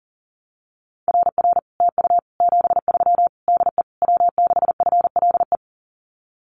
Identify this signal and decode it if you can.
Morse code, heard as a keyed tone, about 43 wpm, 700 Hz. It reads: RR TU 73 DE W6FLE
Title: Common QSO Phrases Encore 40wpm